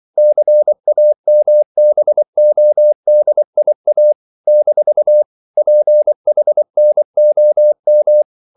Morse code, heard as a keyed tone, about 24 wpm, 600 Hz. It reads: CAMBODIA - PHNOM
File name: CW / Practice / NationalCapitals24